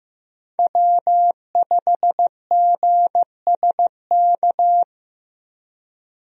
Morse code, heard as a keyed tone, about 15 wpm, 700 Hz. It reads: W5GSK
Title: USA Call Signs - 5 Characters - Repeated 2x - 15wpm